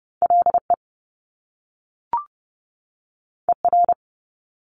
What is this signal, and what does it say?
Morse code, heard as a keyed tone, about 30 wpm, 700 Hz. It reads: LE ER
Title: Top Two Letter Combinations in Words 30wpm